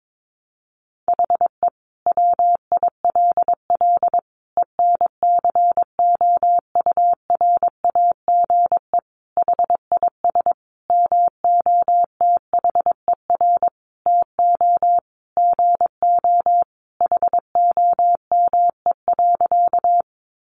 Morse code, heard as a keyed tone, about 22 wpm, 700 Hz. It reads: HE WILL ENCOURAGE HIS MOTHER TO GO HOME.
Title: Sentences from Top 1400 Words 22wpm